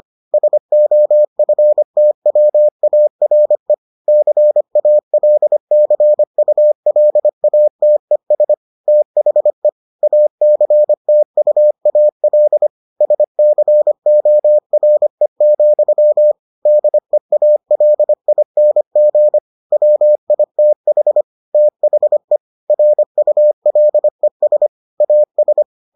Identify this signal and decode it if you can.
Morse code, heard as a keyed tone, about 25 wpm, 600 Hz. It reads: SOFTWARE CALCULATES THE ACTUAL SCORE, DEALING WITH THE RULES AS